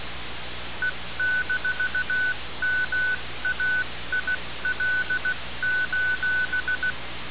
Morse code, heard as a keyed tone, about 16 wpm, 1500 Hz, with noise. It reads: E-MAIL:
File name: radiocap / REC z